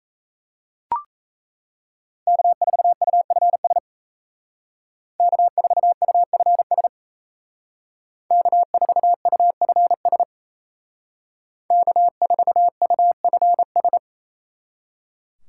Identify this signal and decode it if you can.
Morse code, heard as a keyed tone, about 36 wpm, 700 Hz. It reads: K4UFS K4UFS K4UFS K4UFS